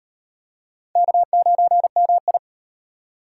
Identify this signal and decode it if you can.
Morse code, heard as a keyed tone, about 37 wpm, 700 Hz. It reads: K9MI